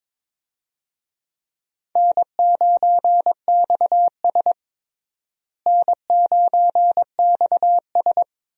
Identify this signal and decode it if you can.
Morse code, heard as a keyed tone, about 22 wpm, 700 Hz. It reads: N9XS N9XS
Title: USA Call Signs - 4 Characters - Repeated 3x - 22wpm